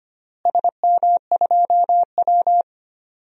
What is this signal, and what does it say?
Morse code, heard as a keyed tone, about 25 wpm, 700 Hz. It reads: SM2W
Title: DXCC-100 Call Signs - 4 Characters - Repeated 3x - 25wpm